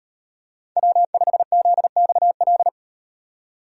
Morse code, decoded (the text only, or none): W5ZXL